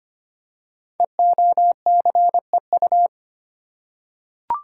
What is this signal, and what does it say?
Morse code, heard as a keyed tone, about 25 wpm, 700 Hz. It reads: EOCEU
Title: Code Groups - 5 Letter:Numbers - 25wpm